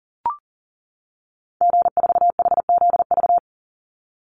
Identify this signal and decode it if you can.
Morse code, heard as a keyed tone, about 38 wpm, 700 Hz. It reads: G4HZV
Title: Call Signs - 40wpm